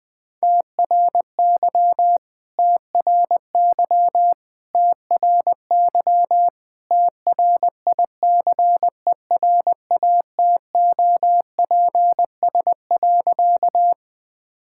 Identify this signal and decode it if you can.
Morse code, heard as a keyed tone, about 20 wpm, 700 Hz. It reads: TRY TRY TRY TRICERATOPS.
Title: All-Age Appropriate Jokes - 20wpm